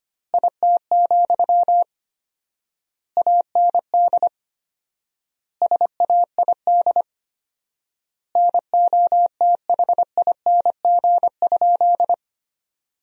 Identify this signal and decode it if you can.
Morse code, heard as a keyed tone, about 25 wpm, 700 Hz. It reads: IT, AND SAID NOTHING?
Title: 4x Word-Spacing - Sentences from Top 500 Words - 25wpm